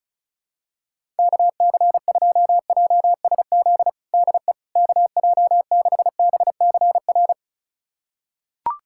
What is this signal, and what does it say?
Morse code, heard as a keyed tone, about 36 wpm, 700 Hz. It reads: KC2JSZ DE KJ6BCR
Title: USA Callsign DE Callsign - 6 Characters - 35wpm